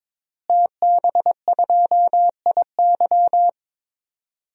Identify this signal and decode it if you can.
Morse code, heard as a keyed tone, about 22 wpm, 700 Hz. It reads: TB2IY
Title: DXCC-100 Call Signs - 5 Characters - Repeated 2x - 22wpm